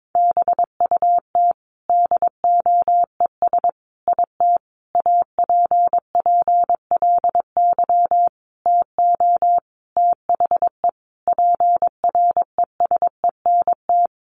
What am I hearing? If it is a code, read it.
Morse code, heard as a keyed tone, about 22 wpm, 700 Hz. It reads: BUT DOES IT APPLY TO THE PRESENT
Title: Sentences from Top 1200 Words 22wpm